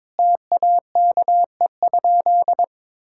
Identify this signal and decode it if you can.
Morse code, heard as a keyed tone, about 22 wpm, 700 Hz. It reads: TAKE?
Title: Sentences from Top 1000 Words - Review Part 6 - 22wpm